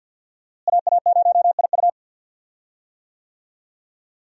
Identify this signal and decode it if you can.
Morse code, heard as a keyed tone, about 46 wpm, 700 Hz. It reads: AA0IU
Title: USA Call Signs - 5 Characters - Repeated 2x - 50wpm